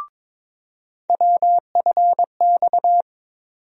Morse code, heard as a keyed tone, about 22 wpm, 700 Hz. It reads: WFX
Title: USA Call Sign Postfixes - 3 Letters 22wpm